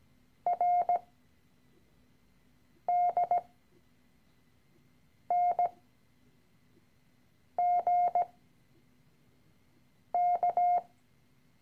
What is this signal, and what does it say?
Morse code, heard as a keyed tone, about 17 wpm, spaced as for 4 wpm, 710 Hz. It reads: RDNGK
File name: CW-Online-Kurs / Anfänger / Lektion 013